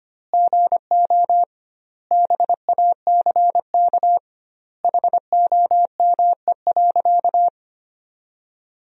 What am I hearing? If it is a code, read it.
Morse code, heard as a keyed tone, about 25 wpm, 700 Hz. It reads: GO BACK HOME.
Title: 2x Word-Spacing - Sentences from Top 500 Words - 25wpm